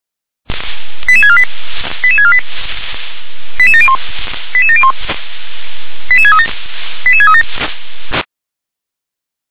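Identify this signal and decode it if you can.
Five-tone selective call (ZVEI), heard as quick runs of five tones, tones of about 70 ms. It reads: 88648 88648 88701 88701 88637 88637
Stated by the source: Received by SDR.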